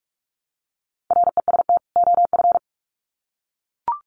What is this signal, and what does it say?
Morse code, heard as a keyed tone, about 44 wpm, 700 Hz. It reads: REST OF
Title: Sets of 2 Words 45wpm